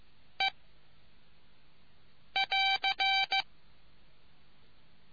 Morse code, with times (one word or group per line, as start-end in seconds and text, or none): 0.4-0.5 E
2.4-3.4 <AR>